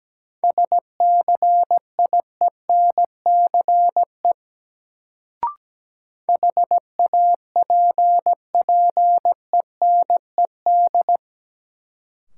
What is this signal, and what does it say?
Morse code, heard as a keyed tone, about 17 wpm, 700 Hz. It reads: SCIENCE HAPPENED
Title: Sentences from Top 1000 Words - Review Part 2 - New Word Drill - 17wpm